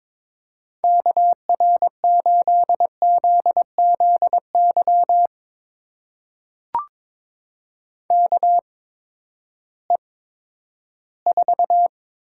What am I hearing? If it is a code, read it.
Morse code, heard as a keyed tone, about 22 wpm, spaced as for 12 wpm, 700 Hz. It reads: KR8ZZY KE4
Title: USA Call Signs - 6 Characters - 8x Character Spacing - 22wpm